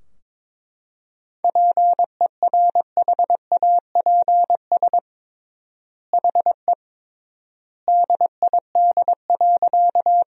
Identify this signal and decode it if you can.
Morse code, heard as a keyed tone, about 22 wpm, 700 Hz. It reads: PERHAPS HE DID.
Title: 3x Word-Spacing - Sentences from Top 500 Words - 22wpm